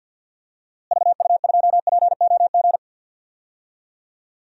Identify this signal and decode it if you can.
Morse code, heard as a keyed tone, about 52 wpm, 700 Hz. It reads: VU2POG
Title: DXCC-100 Call Signs - 6 Characters - Repeated 3x - 50wpm